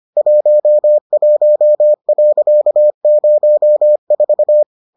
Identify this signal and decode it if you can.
Morse code, heard as a keyed tone, about 25 wpm, 600 Hz. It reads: 11.04